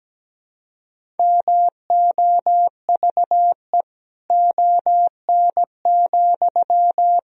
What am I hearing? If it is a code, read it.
Morse code, heard as a keyed tone, about 17 wpm, 700 Hz. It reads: MOVE ON,
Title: Sentences from Top 500 Words and Abbreviations 17wpm